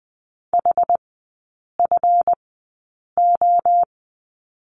Morse code, heard as a keyed tone, about 20 wpm, spaced as for 8 wpm, 700 Hz. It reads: HFO
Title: 178 Lesson 29e - Slash - Three Characters 20wpm